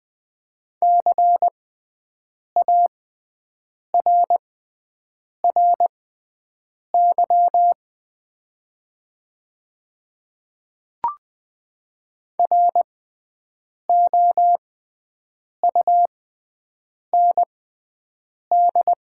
Morse code, heard as a keyed tone, about 20 wpm, spaced as for 7 wpm, 700 Hz. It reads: CARRY ROUND
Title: 6x Character Spacing - Top 300 Words - 20wpm